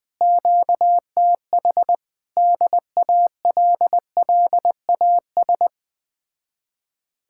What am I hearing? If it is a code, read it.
Morse code, heard as a keyed tone, about 20 wpm, 700 Hz. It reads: QTH DALLAS
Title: Speed-Racing Common QSO Phrases - 17wpm